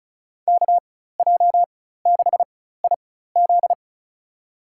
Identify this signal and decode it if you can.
Morse code, heard as a keyed tone, about 34 wpm, spaced as for 16 wpm, 700 Hz. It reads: KJ6IZ
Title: USA Call Signs - 5 Characters - 4x Character Spacing - 35wpm